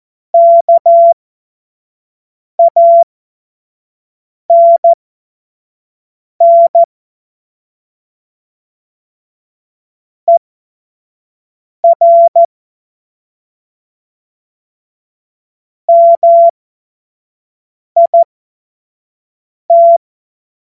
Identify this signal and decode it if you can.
Morse code, heard as a keyed tone, about 14 wpm, spaced as for 5 wpm, 680 Hz. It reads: KANN ER MIT
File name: Morsekurs / Witze / Witz 14. Dezember 2021, 25 F70